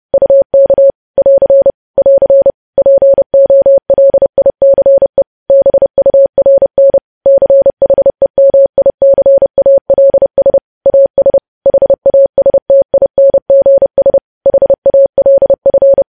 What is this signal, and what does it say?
Morse code, heard as a keyed tone, about 30 wpm, 550 Hz. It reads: UK <AR> <AR> POLICE BURN CHEMICALS AS HASTINGS HALF
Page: news headlines in morse code at 30 wpm